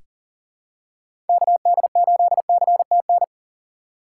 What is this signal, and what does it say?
Morse code, heard as a keyed tone, about 40 wpm, 700 Hz. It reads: KD8CTN
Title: USA Call Signs - 2x Character Spacing - 40wpm